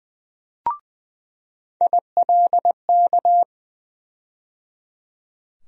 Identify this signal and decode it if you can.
Morse code, heard as a keyed tone, about 21 wpm, 700 Hz. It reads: ILK
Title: USA Call Sign Postfixes - 3 Letters 20wpm